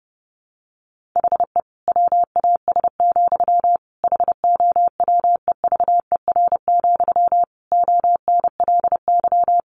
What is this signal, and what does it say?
Morse code, heard as a keyed tone, about 30 wpm, 700 Hz. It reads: HE WAS, HOWEVER, ONLY
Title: Sentences from Top 1100 Words 30wpm